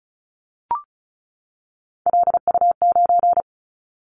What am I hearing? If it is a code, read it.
Morse code, heard as a keyed tone, about 36 wpm, 700 Hz. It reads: LU9
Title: Callsign Prefixes for Top 50 DXCC - 35wpm